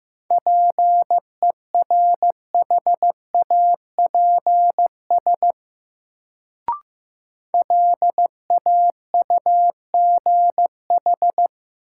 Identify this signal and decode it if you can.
Morse code, heard as a keyed tone, about 15 wpm, 700 Hz. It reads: PERHAPS LAUGH
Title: Sentences from Top 500 Words - New Word Drill - 15wpm